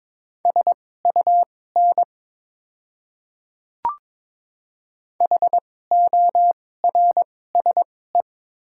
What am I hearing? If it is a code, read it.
Morse code, heard as a keyed tone, about 22 wpm, spaced as for 16 wpm, 700 Hz. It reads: SUN HORSE
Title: 2x Character Spacing - Top 300 Words - 22wpm